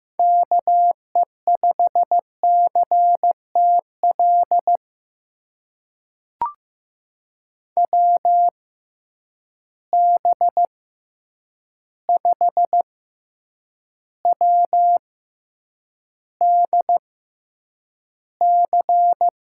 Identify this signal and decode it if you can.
Morse code, heard as a keyed tone, about 15 wpm, spaced as for 8 wpm, 700 Hz. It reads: KE5CTL WB5WDC
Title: USA Call Signs - 6 Characters - 6x Character Spacing - 15wpm